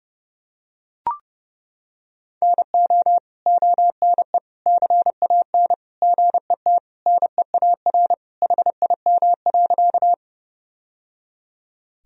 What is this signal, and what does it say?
Morse code, heard as a keyed tone, about 30 wpm, 700 Hz. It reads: NO ONE CAN GET NEAR HIM.
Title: Sentences from Top 1000 Words - Review Part 1 - 30wpm